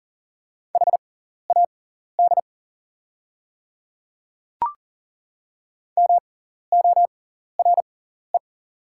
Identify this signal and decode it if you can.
Morse code, heard as a keyed tone, about 40 wpm, spaced as for 14 wpm, 690 Hz. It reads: HAD MORE
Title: 6x Character Spacing - Top 100 Words - 40wpm